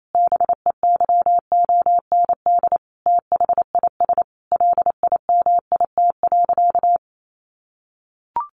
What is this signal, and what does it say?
Morse code, heard as a keyed tone, about 28 wpm, 700 Hz. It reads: BEYOND THIS LIMIT.
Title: Sentences from Top 1500 Words 28wpm